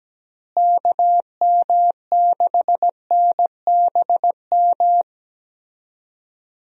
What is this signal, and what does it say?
Morse code, heard as a keyed tone, about 17 wpm, 700 Hz. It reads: KM6NBM
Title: USA Call Signs - 6 Characters - Repeated 2x - 17wpm